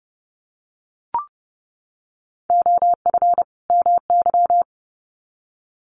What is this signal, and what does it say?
Morse code, heard as a keyed tone, about 30 wpm, 700 Hz. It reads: OF MY
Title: Sets of 2 Words 30wpm